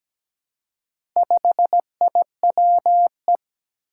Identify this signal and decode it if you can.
Morse code, heard as a keyed tone, about 17 wpm, 700 Hz. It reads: 5IWE
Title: Four Challenging Letter-Number Combinations 17wpm